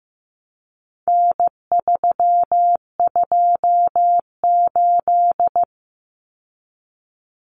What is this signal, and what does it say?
Morse code, heard as a keyed tone, about 15 wpm, 700 Hz. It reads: N328
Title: Common Cut Numbers - 4-Digit 15wpm